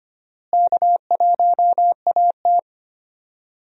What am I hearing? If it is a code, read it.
Morse code, heard as a keyed tone, about 25 wpm, 700 Hz. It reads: K1AT
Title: USA Call Signs - 4 Characters - Repeated 3x - 25wpm